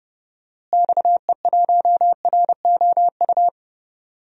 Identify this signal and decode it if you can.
Morse code, heard as a keyed tone, about 30 wpm, 700 Hz. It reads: XE1ROU
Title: DXCC-100 Call Signs - 6 Characters - Repeated 3x - 30wpm